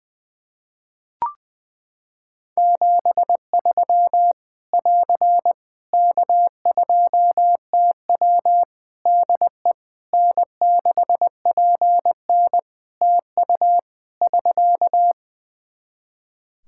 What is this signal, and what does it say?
Morse code, heard as a keyed tone, about 20 wpm, 700 Hz. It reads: 73 <AR> K2TW DE N6PN TU <SK>
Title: Practice QSO - Exchange 4 - Ending - 20wpm